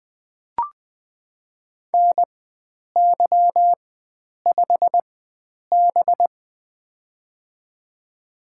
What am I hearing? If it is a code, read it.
Morse code, heard as a keyed tone, about 20 wpm, spaced as for 9 wpm, 700 Hz. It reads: NY5B
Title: USA Call Signs - 4x Character Spacing - 20wpm